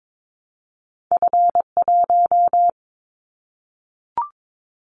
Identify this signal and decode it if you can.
Morse code, heard as a keyed tone, about 22 wpm, 700 Hz. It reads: F1
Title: Callsign Prefixes for Top 50 DXCC - 22wpm